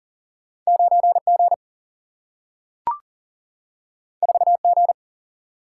Morse code, heard as a keyed tone, about 40 wpm, 700 Hz. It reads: 9 G 4 G
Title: Single-Letter Single-Number 40wpm